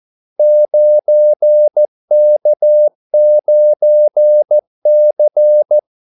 Morse code, heard as a keyed tone, about 14 wpm, 600 Hz. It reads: 9K9C
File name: Callsigns-14wpm